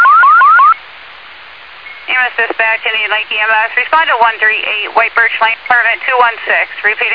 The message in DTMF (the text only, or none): none